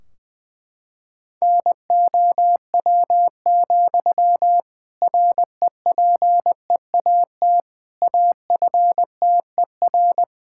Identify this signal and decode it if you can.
Morse code, heard as a keyed tone, about 21 wpm, 700 Hz. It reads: NOW, REPEAT AFTER
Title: Sentences from Top 1600 Words 20wpm